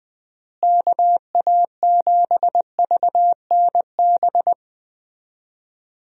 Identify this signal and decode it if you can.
Morse code, heard as a keyed tone, about 20 wpm, 700 Hz. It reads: KA7VNB
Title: USA Call Signs - 6 Characters - Repeated 3x - 20wpm